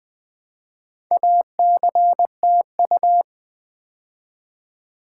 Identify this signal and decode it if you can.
Morse code, heard as a keyed tone, about 20 wpm, 700 Hz. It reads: ACTU
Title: Build A Word - 2 Letters At A Time - 6-Letter Words - 20wpm